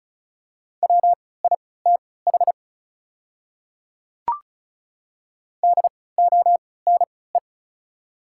Morse code, heard as a keyed tone, about 36 wpm, spaced as for 20 wpm, 700 Hz. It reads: WITH DONE
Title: 3x Character Spacing - Top 500 Words - 35wpm